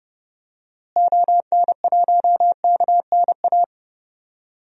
Morse code, heard as a keyed tone, about 30 wpm, 700 Hz. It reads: ON1KNA
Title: DXCC-100 Call Signs - 6 Characters - Repeated 3x - 30wpm